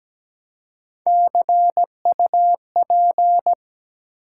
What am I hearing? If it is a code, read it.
Morse code, heard as a keyed tone, about 17 wpm, 700 Hz. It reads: CUP
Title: Three Letter Words - 17wpm